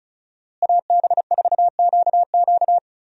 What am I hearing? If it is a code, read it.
Morse code, heard as a keyed tone, about 34 wpm, 700 Hz. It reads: AB4QQ
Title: USA Call Signs - 5 Characters - Repeated 2x - 35wpm